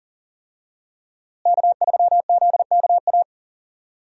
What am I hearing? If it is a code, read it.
Morse code, heard as a keyed tone, about 40 wpm, 700 Hz. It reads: K3ZKA
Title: USA Call Signs - 5 Characters - Repeated 2x - 40wpm